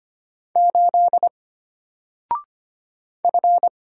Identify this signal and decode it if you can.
Morse code, heard as a keyed tone, about 25 wpm, 700 Hz. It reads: 8 F
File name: Single-Letter-Number-25wpm